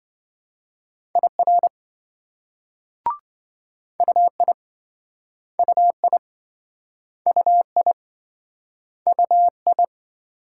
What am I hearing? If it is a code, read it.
Morse code, heard as a keyed tone, about 27 wpm, 700 Hz. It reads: IR UI UI UI UI